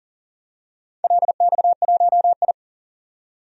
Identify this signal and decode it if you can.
Morse code, heard as a keyed tone, about 42 wpm, 700 Hz. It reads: LX1I
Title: DXCC-100 Call Signs - 4 Characters - 40wpm